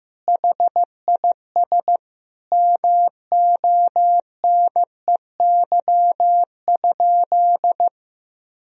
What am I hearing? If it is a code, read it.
Morse code, heard as a keyed tone, about 15 wpm, 700 Hz. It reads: HIS MONEY?
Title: Sentences from Top 400 Words and Abbreviations 15wpm